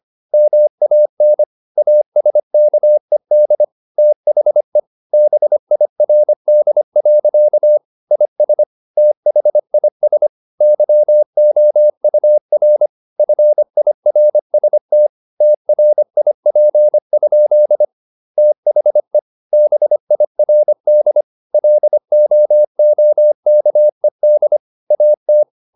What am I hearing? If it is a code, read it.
Morse code, heard as a keyed tone, about 25 wpm, 600 Hz. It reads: MAN ASKED THE BIRD. IS THIS YOUR FIRST TRIP? THE BIRD LOOKED AT
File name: the old man and the sea part 1 25wpm